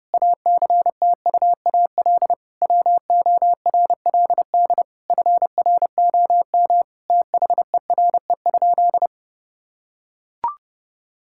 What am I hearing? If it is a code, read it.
Morse code, heard as a keyed tone, about 32 wpm, 700 Hz. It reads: ACTUAL WORLD FROM THERE?
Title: Sentences from Top 1000 Words - Review Part 5 - 30wpm